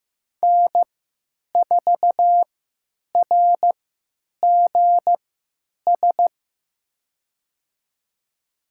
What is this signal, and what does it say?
Morse code, heard as a keyed tone, about 15 wpm, spaced as for 9 wpm, 700 Hz. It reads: N4RGS
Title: USA Call Signs - 3x Character Spacing - 15wpm